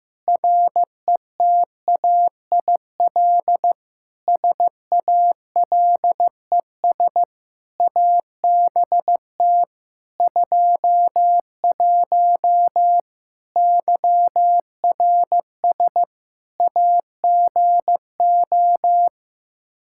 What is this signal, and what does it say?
Morse code, heard as a keyed tone, about 15 wpm, 700 Hz. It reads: RETAIL SALES ABT 21 YRS AGO
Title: Practice QSO - Exchange 3 - More Info - 15wpm